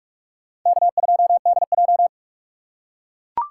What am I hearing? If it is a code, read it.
Morse code, heard as a keyed tone, about 44 wpm, 700 Hz. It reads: K2DJ